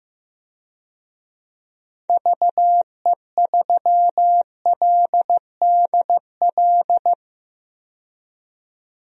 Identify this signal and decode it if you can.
Morse code, heard as a keyed tone, about 15 wpm, 700 Hz. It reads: VE3LDL
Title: DXCC-100 Call Signs - 6 Characters - 15wpm